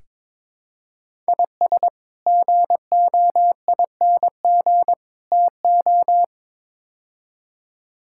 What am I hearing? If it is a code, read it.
Morse code, heard as a keyed tone, about 21 wpm, 700 Hz. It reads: IS GOING TO
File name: practice / 22wpm / Build A Phrase - 5-Word Phrases - 22wpm